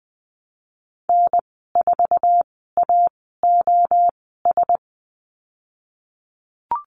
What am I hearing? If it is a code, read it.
Morse code, heard as a keyed tone, about 20 wpm, spaced as for 15 wpm, 700 Hz. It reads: N4AOS
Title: 055 Lesson 10g - Letter r - Call Signs 20wpm